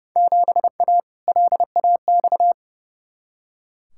Morse code, heard as a keyed tone, about 29 wpm, 700 Hz. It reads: 7A LAX
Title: Field Day Exchanges - 30wpm